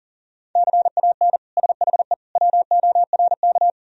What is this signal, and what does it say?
Morse code, heard as a keyed tone, about 40 wpm, 700 Hz. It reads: CAN SHE WORK
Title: Sentences from Top 1000 Words - Review Part 4 - 40wpm